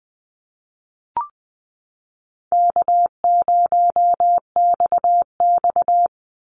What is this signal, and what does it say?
Morse code, heard as a keyed tone, about 20 wpm, 700 Hz. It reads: K0XX
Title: Easy Call Signs Repeated 2x - 20wpm